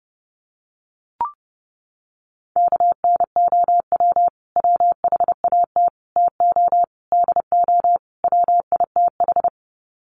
Morse code, heard as a keyed tone, about 30 wpm, 700 Hz. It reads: KNOW WHAT TO DO WITH
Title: Sets of 5 Words 30wpm